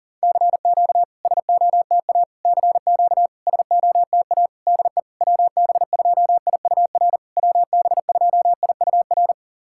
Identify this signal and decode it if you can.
Morse code, heard as a keyed tone, about 40 wpm, 690 Hz. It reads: CQ SOTA CQ SOTA DE WB2IUR WB2IUR
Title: SOTA QSO Practice 40wpm